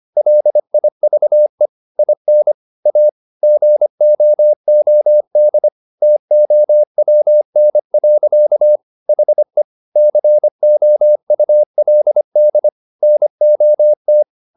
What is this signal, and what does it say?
Morse code, heard as a keyed tone, about 25 wpm, 600 Hz. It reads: LIVE IN A GOOD TOWN. HE COULD NOT